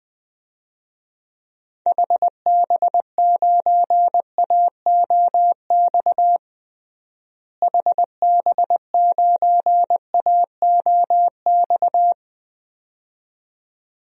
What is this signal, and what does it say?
Morse code, heard as a keyed tone, about 21 wpm, 690 Hz. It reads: HB9AOX HB9AOX
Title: DXCC-100 Call Signs - 6 Characters - Repeated 2x - 20wpm